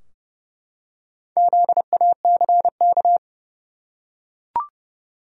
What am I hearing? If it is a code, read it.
Morse code, heard as a keyed tone, about 32 wpm, 700 Hz. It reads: ZACK
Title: USA Common Names - Max 5 Letters - 30wpm